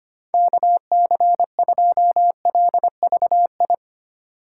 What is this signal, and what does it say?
Morse code, heard as a keyed tone, about 25 wpm, 700 Hz. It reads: KC2LVI
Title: USA Call Signs - 6 Characters - 4x Character Spacing - 25wpm